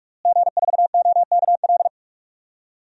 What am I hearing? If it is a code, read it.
Morse code, heard as a keyed tone, about 46 wpm, 700 Hz. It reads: G4OKL
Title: DXCC-100 Call Signs - 5 Characters - Repeated 3x - 45wpm